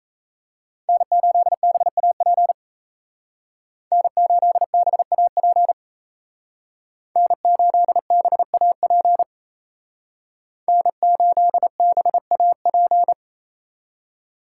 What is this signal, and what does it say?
Morse code, heard as a keyed tone, about 33 wpm, 700 Hz. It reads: N8BAP N8BAP N8BAP N8BAP
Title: Speed Racing - USA 1x3 Call Signs 28wpm